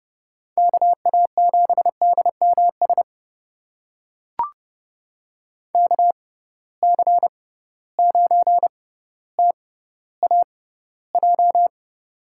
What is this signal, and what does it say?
Morse code, heard as a keyed tone, about 30 wpm, spaced as for 15 wpm, 700 Hz. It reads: KA7DMS KC9TAJ
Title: USA Call Signs - 6 Characters - 6x Character Spacing - 30wpm